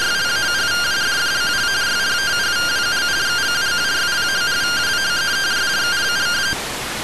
Two-tone teleprinter signal, about 170 Hz shift.